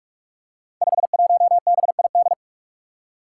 Morse code, heard as a keyed tone, about 44 wpm, 700 Hz. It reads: 51BID
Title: Five Challenging Letter-Number Combinations 45wpm